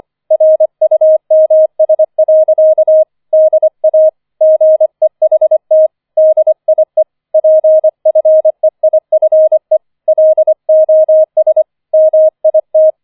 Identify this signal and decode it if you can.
Morse code, heard as a keyed tone, about 25 wpm, 630 Hz. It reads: RUMS. DA GEHT DIE PFEIFE LOS MIT
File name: 004 Vierter Streich 25wpm